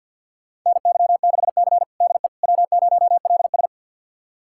Morse code, heard as a keyed tone, about 48 wpm, 700 Hz. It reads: NY6C DE W0LS
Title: USA Callsign DE Callsign - 4 Characters - 50wpm